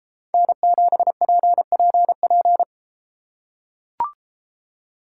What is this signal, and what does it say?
Morse code, heard as a keyed tone, about 31 wpm, 700 Hz. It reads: N7PPP